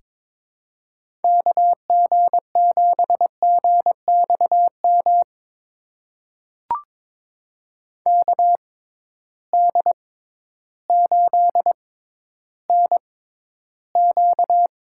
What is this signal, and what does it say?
Morse code, heard as a keyed tone, about 22 wpm, spaced as for 12 wpm, 700 Hz. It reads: KG7GXM KD8NQ
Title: USA Call Signs - 6 Characters - 6x Character Spacing - 22wpm